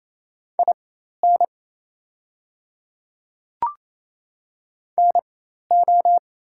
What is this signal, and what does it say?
Morse code, heard as a keyed tone, about 28 wpm, spaced as for 13 wpm, 700 Hz. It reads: IN NO
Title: 4x Character Spacing - Top 100 Words - 28wpm